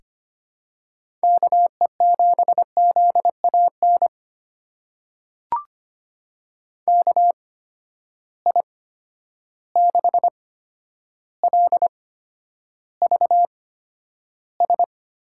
Morse code, heard as a keyed tone, about 25 wpm, spaced as for 11 wpm, 700 Hz. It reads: KE7ZAN KI6LVS
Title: USA Call Signs - 6 Characters - 8x Character Spacing - 25wpm